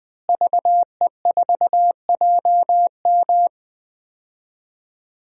Morse code, heard as a keyed tone, about 20 wpm, 700 Hz. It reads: VE4JM
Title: DXCC-100 Call Signs - 5 Characters - Repeated 3x - 20wpm